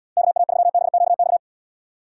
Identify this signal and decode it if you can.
Morse code, heard as a keyed tone, about 78 wpm, 700 Hz. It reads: KA3BQX